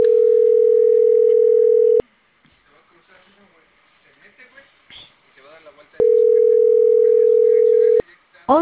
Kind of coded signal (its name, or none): telephone call progress tone